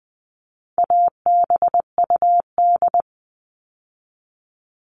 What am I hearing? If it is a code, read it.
Morse code, heard as a keyed tone, about 20 wpm, 700 Hz. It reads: ABUD
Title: All Cut Numbers - 4-Digit 20wpm